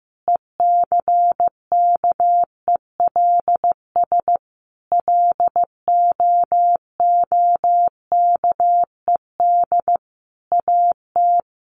Morse code, heard as a keyed tone, about 15 wpm, 700 Hz. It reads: ECKELS LOOKED AT
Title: A Sound Of Thunder 15wpm